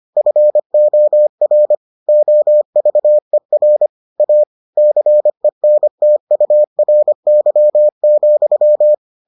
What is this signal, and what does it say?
Morse code, heard as a keyed tone, about 25 wpm, 600 Hz. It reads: FOR OVER A CENTURY,